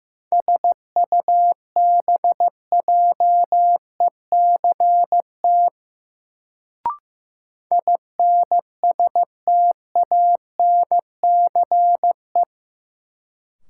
Morse code, heard as a keyed tone, about 15 wpm, 700 Hz. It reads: SUBJECT INSTANCE
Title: Sentences from Top 800 Words - New Word Drill - 15wpm